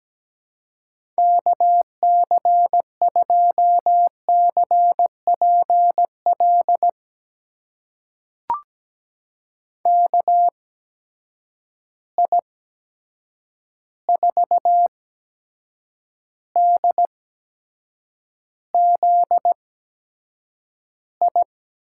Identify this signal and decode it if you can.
Morse code, heard as a keyed tone, about 17 wpm, spaced as for 7 wpm, 700 Hz. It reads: KC2CPL KI4DZI
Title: USA Call Signs - 8x Character Spacing - 17wpm